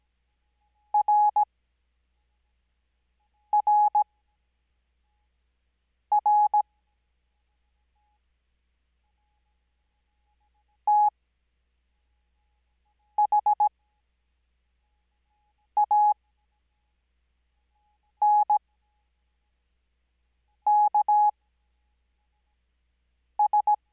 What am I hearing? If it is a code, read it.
Morse code, heard as a keyed tone, about 17 wpm, spaced as for 4 wpm, 840 Hz. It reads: RRR THANKS
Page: Farnsworth Spacing: 15WPM Character speed Spaced at 5WPM.